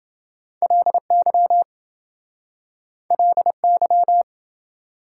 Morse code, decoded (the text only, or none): LY LY